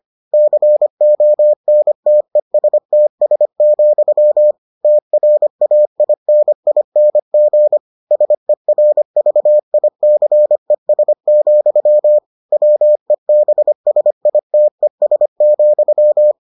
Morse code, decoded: CONTESTS, TRAINING SERVICES, WEBSITES,

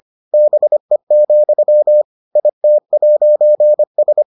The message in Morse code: BE, IT'S

